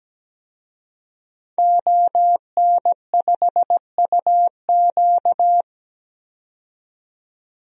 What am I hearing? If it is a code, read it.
Morse code, heard as a keyed tone, about 17 wpm, 700 Hz. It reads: ON5UQ
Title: DXCC-100 Call Signs - 5 Characters - 17wpm